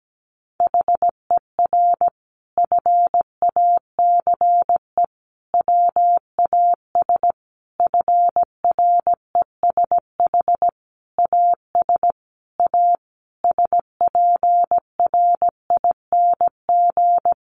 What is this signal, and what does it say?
Morse code, heard as a keyed tone, about 17 wpm, 700 Hz. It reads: HER FACE WAS FRESH AS A SPRING
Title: Sentences from Top 1200 Words 17wpm